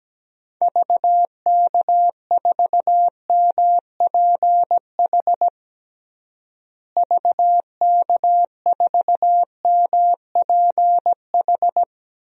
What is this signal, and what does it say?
Morse code, heard as a keyed tone, about 17 wpm, 700 Hz. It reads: VK4MPH VK4MPH
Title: DXCC-100 Call Signs - 6 Characters - Repeated 3x - 17wpm